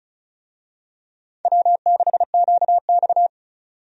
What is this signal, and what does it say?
Morse code, heard as a keyed tone, about 36 wpm, 700 Hz. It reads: W6QX
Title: USA Call Signs - 4 Characters - 35wpm